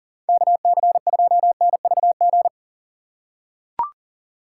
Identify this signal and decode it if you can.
Morse code, heard as a keyed tone, about 40 wpm, 700 Hz. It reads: KC2NVG
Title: USA Call Signs - 6x Character Spacing - 40wpm